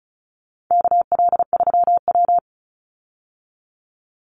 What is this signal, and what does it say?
Morse code, heard as a keyed tone, about 34 wpm, 700 Hz. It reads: KL3W
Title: Call Signs Repeated 2x - 35wpm